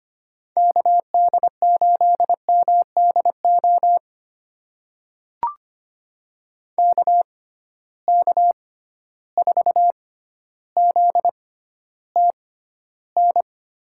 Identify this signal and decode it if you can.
Morse code, heard as a keyed tone, about 25 wpm, spaced as for 13 wpm, 700 Hz. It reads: KD8MDO KK4ZTN
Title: USA Call Signs - 6 Characters - 6x Character Spacing - 25wpm